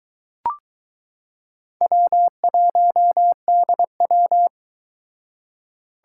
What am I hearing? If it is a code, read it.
Morse code, heard as a keyed tone, about 22 wpm, 700 Hz. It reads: W1DW